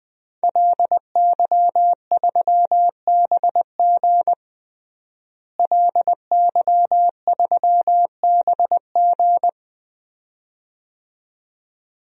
Morse code, heard as a keyed tone, about 21 wpm, 700 Hz. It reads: LY3BG LY3BG
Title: DXCC-100 Call Signs - 5 Characters - Repeated 2x - 20wpm